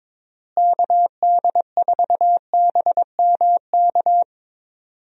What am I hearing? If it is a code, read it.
Morse code, heard as a keyed tone, about 22 wpm, 700 Hz. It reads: KD4BMK
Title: USA Call Signs - 6 Characters - Repeated 3x - 22wpm